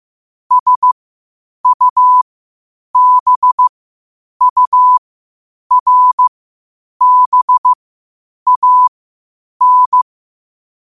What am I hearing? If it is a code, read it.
Morse code, heard as a keyed tone, about 15 wpm, spaced as for 8 wpm, 1000 Hz. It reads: SUBURBAN
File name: qotdfiles / 8wpm